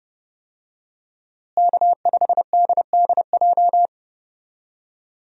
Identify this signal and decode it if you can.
Morse code, heard as a keyed tone, about 30 wpm, 700 Hz. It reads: K5DDJ
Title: USA Call Signs - 5 Characters - Repeated 2x - 30wpm